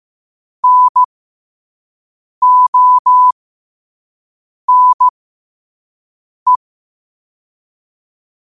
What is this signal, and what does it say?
Morse code, heard as a keyed tone, about 15 wpm, spaced as for 5 wpm, 1000 Hz. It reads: NONE
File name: qotdfiles / 5wpm